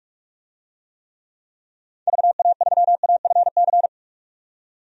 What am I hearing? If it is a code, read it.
Morse code, heard as a keyed tone, about 41 wpm, 700 Hz. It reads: VA3AFC
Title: DXCC-100 Call Signs - 6 Characters - 45wpm